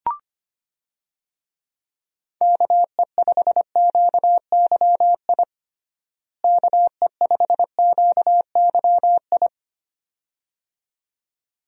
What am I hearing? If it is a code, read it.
Morse code, heard as a keyed tone, about 25 wpm, 700 Hz. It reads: KE5QYI KE5QYI
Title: USA Call Signs - 6 Characters - Repeated 2x - 25wpm